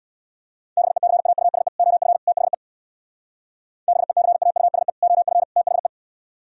Morse code, heard as a keyed tone, about 62 wpm, 700 Hz. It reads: BECAUSE OF THE BECAUSE OF THE
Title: Speed-Racing - Sets of 3 Words 50wpm